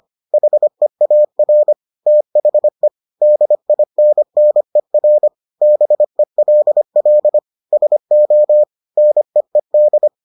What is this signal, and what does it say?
Morse code, heard as a keyed tone, about 25 wpm, 600 Hz. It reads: HEAR THE DINNER BELL SO NEED